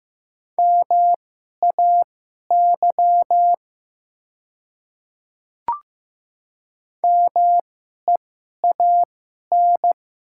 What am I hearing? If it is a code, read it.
Morse code, heard as a keyed tone, about 15 wpm, spaced as for 11 wpm, 700 Hz. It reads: MAY MEAN
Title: 2x Character Spacing - Top 300 Words - 15wpm